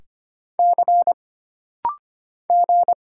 Morse code, heard as a keyed tone, about 25 wpm, 700 Hz. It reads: C G
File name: Single Character (Letter:Number:Punctuation:Prosign) - Rapid-Fire - 25wpm